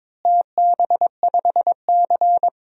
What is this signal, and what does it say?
Morse code, heard as a keyed tone, about 22 wpm, 700 Hz. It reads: TB5C
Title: DXCC-100 Call Signs - 4 Characters - Repeated 2x - 22wpm